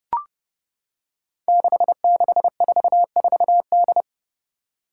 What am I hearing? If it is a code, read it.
Morse code, heard as a keyed tone, about 28 wpm, 700 Hz. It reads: 6644D
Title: Five Challenging Letter-Number Combinations 30wpm